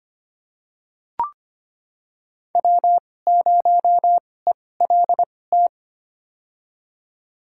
Morse code, heard as a keyed tone, about 25 wpm, spaced as for 18 wpm, 700 Hz. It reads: W0ELT
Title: USA Call Signs - 5 Characters - 2x Character Spacing - 25wpm